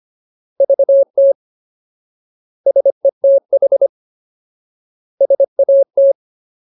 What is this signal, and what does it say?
Morse code, heard as a keyed tone, about 25 wpm, 550 Hz. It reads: VT SETH SAT
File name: Ultimate Word List / 25WPM Ultimate